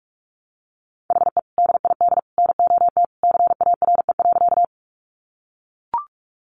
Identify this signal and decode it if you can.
Morse code, heard as a keyed tone, about 44 wpm, 700 Hz. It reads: HE DID NOT CARE.
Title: Sentences from Top 600 Words 45wpm